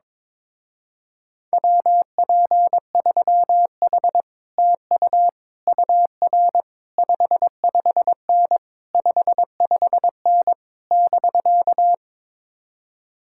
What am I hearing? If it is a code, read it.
Morse code, heard as a keyed tone, about 22 wpm, 700 Hz. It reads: WP3H TU UR 55N 55N <BK>
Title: Speed-Racing Common QSO Phrases - 22wpm